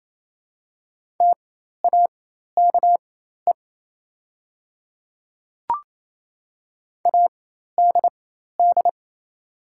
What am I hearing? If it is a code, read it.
Morse code, heard as a keyed tone, about 28 wpm, spaced as for 13 wpm, 690 Hz. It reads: TAKE ADD
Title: 4x Character Spacing - Top 200 Words - 28wpm